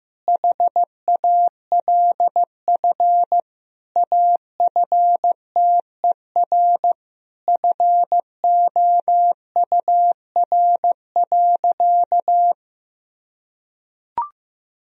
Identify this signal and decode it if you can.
Morse code, heard as a keyed tone, about 16 wpm, 700 Hz. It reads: HALF AFTER FOUR.
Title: Sentences from Top 1000 Words - Review Part 1 - 15wpm